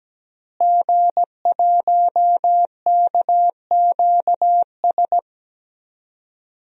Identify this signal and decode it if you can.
Morse code, heard as a keyed tone, about 17 wpm, 700 Hz. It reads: G1KQS
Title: DXCC-100 Call Signs - 5 Characters - Repeated 2x - 17wpm